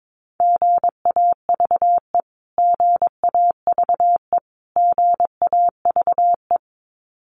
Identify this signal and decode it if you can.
Morse code, heard as a keyed tone, about 22 wpm, 700 Hz. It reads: GAVE GAVE GAVE
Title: Top 500 Words Repeated 22wpm